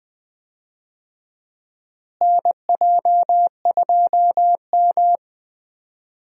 Morse code, heard as a keyed tone, about 21 wpm, 700 Hz. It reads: NJ2M
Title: USA Call Signs - 4 Characters - Repeated 2x - 20wpm